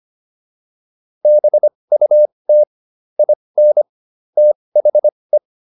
Morse code, heard as a keyed tone, about 25 wpm, spaced as for 20 wpm, 600 Hz. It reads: BUT IN THE